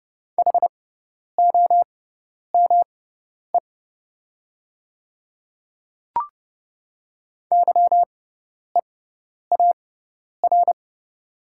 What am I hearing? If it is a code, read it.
Morse code, heard as a keyed tone, about 30 wpm, spaced as for 10 wpm, 690 Hz. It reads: HOME YEAR
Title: 6x Character Spacing - Top 200 Words - 30wpm